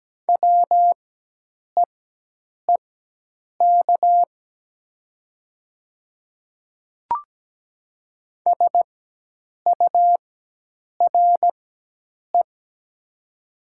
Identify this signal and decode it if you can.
Morse code, heard as a keyed tone, about 17 wpm, spaced as for 8 wpm, 700 Hz. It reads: WEEK SURE
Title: 4x Character Spacing - Top 500 Words - 17wpm